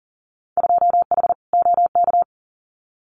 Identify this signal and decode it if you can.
Morse code, heard as a keyed tone, about 40 wpm, 700 Hz. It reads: 2H OK
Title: Winter Field Day Exchanges Encore 40wpm